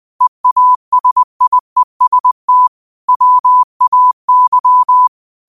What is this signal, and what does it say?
Morse code, heard as a keyed tone, about 20 wpm, 1000 Hz. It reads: EASIEST WAY
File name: qotdfiles / 20wpm